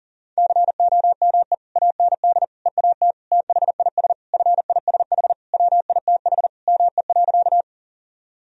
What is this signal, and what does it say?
Morse code, heard as a keyed tone, about 40 wpm, 700 Hz. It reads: COME AND EAT THIS FISH WITH ME.
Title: Sentences from Top 1000 Words - Review Part 1 - 40wpm